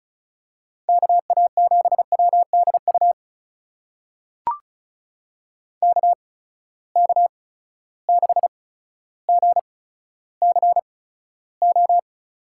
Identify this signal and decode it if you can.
Morse code, heard as a keyed tone, about 36 wpm, spaced as for 15 wpm, 700 Hz. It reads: KA7WDU KK6GCO
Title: USA Call Signs - 6 Characters - 8x Character Spacing - 35wpm